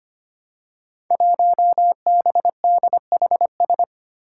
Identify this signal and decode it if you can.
Morse code, heard as a keyed tone, about 25 wpm, 700 Hz. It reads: 1BDHS
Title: Five Challenging Letter-Number Combinations 25wpm